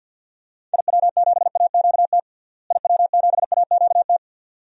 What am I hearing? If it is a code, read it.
Morse code, heard as a keyed tone, about 49 wpm, 700 Hz. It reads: IW7AQT IW7AQT
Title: DXCC-100 Call Signs - 6 Characters - Repeated 2x - 50wpm